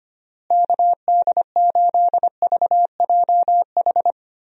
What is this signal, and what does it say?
Morse code, heard as a keyed tone, about 25 wpm, 700 Hz. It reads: KD8VJH
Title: USA Call Signs - 6 Characters - Repeated 2x - 25wpm